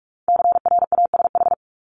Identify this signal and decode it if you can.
Morse code, heard as a keyed tone, about 46 wpm, 700 Hz. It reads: CRASH
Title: Multi-letter Phonemes Repeated 3x 45wpm